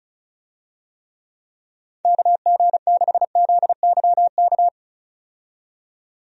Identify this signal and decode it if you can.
Morse code, heard as a keyed tone, about 34 wpm, 700 Hz. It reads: KG6ZYK
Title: USA Call Signs - 6 Characters - 35wpm